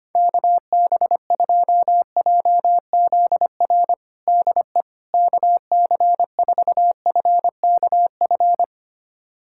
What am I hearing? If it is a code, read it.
Morse code, heard as a keyed tone, about 25 wpm, 700 Hz. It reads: KB2JZR DE KC4FKF
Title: USA Callsign DE Callsign - 6 Characters - 25wpm